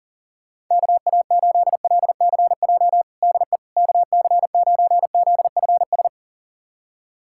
Morse code, heard as a keyed tone, about 37 wpm, 700 Hz. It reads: KA8LCJ DE KC9ZFS
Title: USA Callsign DE Callsign - 6 Characters - 40wpm